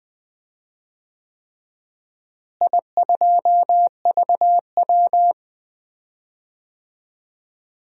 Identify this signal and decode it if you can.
Morse code, heard as a keyed tone, about 20 wpm, 700 Hz. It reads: I2VW
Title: DXCC-100 Call Signs - 4 Characters - Repeated 3x - 20wpm